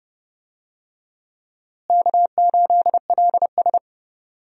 Morse code, heard as a keyed tone, about 30 wpm, 700 Hz. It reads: K8LS
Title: USA Call Signs - 4 Characters - Repeated 2x - 30wpm